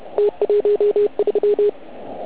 Morse code, heard as a keyed tone, about 30 wpm, 410 Hz. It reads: T13
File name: Archiv / CUC129 / CUC129 pileup